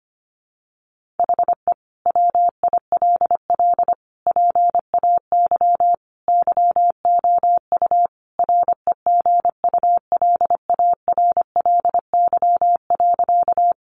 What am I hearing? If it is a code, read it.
Morse code, heard as a keyed tone, about 25 wpm, 700 Hz. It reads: HE WILL PAY YOU REGULARLY.
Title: Sentences from Top 1400 Words 25wpm